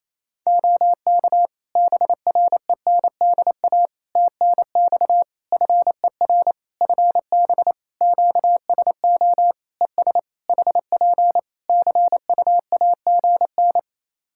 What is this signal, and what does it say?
Morse code, heard as a keyed tone, about 28 wpm, 700 Hz. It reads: OK BRENDA TNX FER FB QSO ES HP CUAGN
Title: Practice QSO - Exchange 4 - Ending - 28wpm